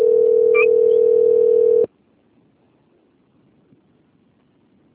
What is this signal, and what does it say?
Ringback tone (North America), 440 + 480 Hz.